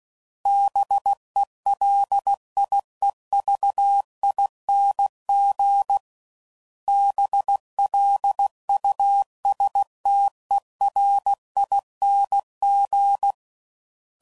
Morse code, heard as a keyed tone, about 16 wpm, 800 Hz. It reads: BELIEVING BLUSTERING
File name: Suffixes / Special INGs / Sngl Ings